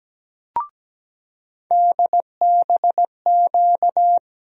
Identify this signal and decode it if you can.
Morse code, heard as a keyed tone, about 17 wpm, 700 Hz. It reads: DBQ
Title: USA Call Sign Postfixes - 3 Letters 17wpm